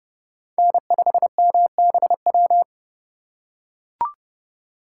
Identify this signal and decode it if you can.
Morse code, heard as a keyed tone, about 30 wpm, 700 Hz. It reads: N5MBW
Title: USA Call Signs - 5 Characters - 4x Character Spacing - 30wpm